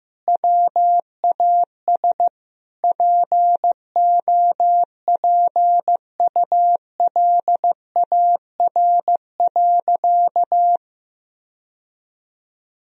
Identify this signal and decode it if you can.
Morse code, heard as a keyed tone, about 15 wpm, 700 Hz. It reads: WAS POPULAR.
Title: Sentences from Top 1000 Words - Review Part 3 - 15wpm